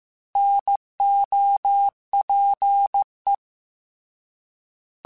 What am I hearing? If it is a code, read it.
Morse code, heard as a keyed tone, about 15 wpm, 800 Hz. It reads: NOPE
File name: Alphabetical / Alphabetical mp3s / Ns @13